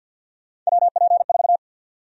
Morse code, heard as a keyed tone, about 52 wpm, 700 Hz. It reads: WP4
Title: USA Call Sign Prefixes - 50wpm